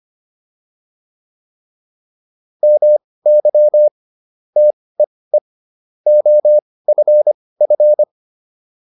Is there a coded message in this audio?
Yes: Morse code from 2.6 to 8.0 s, heard as a keyed tone, about 25 wpm, spaced as for 18 wpm, 600 Hz. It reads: MY TEE OFF